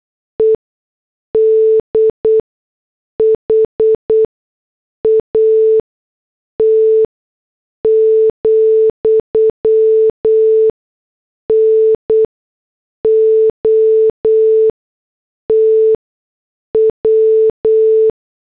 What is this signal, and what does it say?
Morse code, heard as a keyed tone, about 8 wpm, spaced as for 6 wpm, 440 Hz. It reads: EDHAT, NOT W